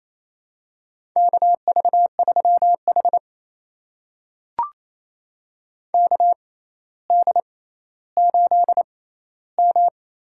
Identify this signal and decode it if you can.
Morse code, heard as a keyed tone, about 28 wpm, spaced as for 14 wpm, 700 Hz. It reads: KV3H KD8M
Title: USA Call Signs - 4 Characters - 6x Character Spacing - 28wpm